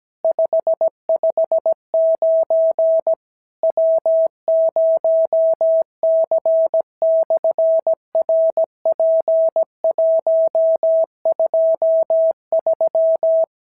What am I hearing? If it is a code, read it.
Morse code, heard as a keyed tone, about 17 wpm, 650 Hz. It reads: 559 W0C/RP123
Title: SOTA QSO Practice 17wpm